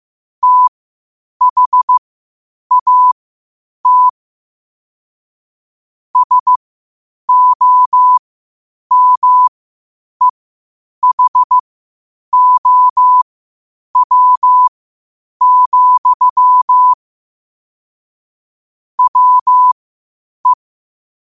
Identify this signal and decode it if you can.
Morse code, heard as a keyed tone, about 15 wpm, spaced as for 8 wpm, 1000 Hz. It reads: THAT SOMEHOW, WE
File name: qotdfiles / 8wpm